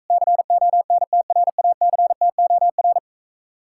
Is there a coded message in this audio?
Yes: Morse code from 0.1 to 3.0 s, heard as a keyed tone, about 46 wpm, 700 Hz. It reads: CONTRACTOR